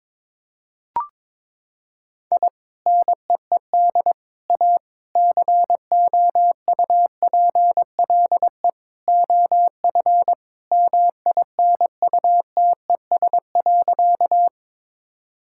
Morse code, heard as a keyed tone, about 21 wpm, 700 Hz. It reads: I NEED A COUPLE OF MINUTES.
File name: practice / 22wpm / Sentences with 3-Word Phrases - 22wpm